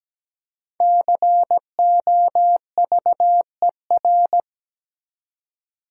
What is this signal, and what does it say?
Morse code, heard as a keyed tone, about 17 wpm, 700 Hz. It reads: COVER
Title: Five Letter Words - 17wpm